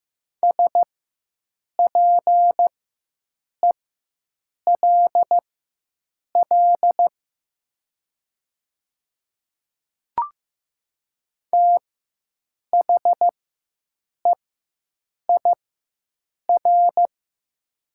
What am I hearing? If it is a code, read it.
Morse code, heard as a keyed tone, about 15 wpm, spaced as for 7 wpm, 700 Hz. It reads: SPELL THEIR
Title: 4x Character Spacing - Top 300 Words - 15wpm